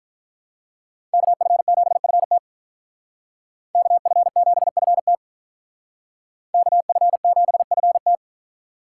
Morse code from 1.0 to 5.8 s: KF7FT KF7FT

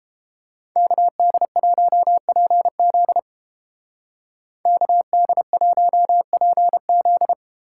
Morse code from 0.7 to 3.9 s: KD1PZ

